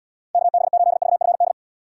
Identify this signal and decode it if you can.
Morse code, heard as a keyed tone, about 78 wpm, 700 Hz. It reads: KB8VFD